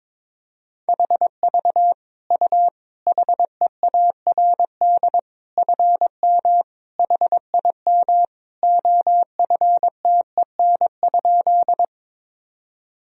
Morse code, heard as a keyed tone, about 22 wpm, 700 Hz. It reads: HV U HEARD FM HIM OFTEN?
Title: Sentences from Top 400 Words and Abbreviations 22wpm